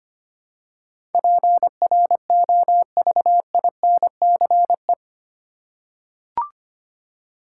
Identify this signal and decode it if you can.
Morse code, heard as a keyed tone, about 25 wpm, 700 Hz. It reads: PROVINCE
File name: 8 Letter Words - First 3 Letters Given - 25wpm